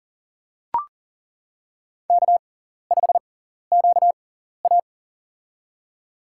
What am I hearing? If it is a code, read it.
Morse code, heard as a keyed tone, about 38 wpm, spaced as for 14 wpm, 700 Hz. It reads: K5QA
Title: USA Call Signs - 4 Characters - 6x Character Spacing - 40wpm